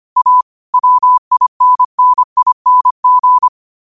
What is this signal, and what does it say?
Morse code, heard as a keyed tone, about 25 wpm, 1000 Hz. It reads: A WINNING
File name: qotdfiles / 25wpm